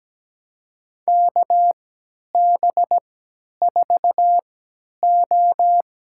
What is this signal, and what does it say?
Morse code, heard as a keyed tone, about 17 wpm, spaced as for 10 wpm, 700 Hz. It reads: KB4O
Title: USA Call Signs - 6 Characters - 3x Character Spacing - 17wpm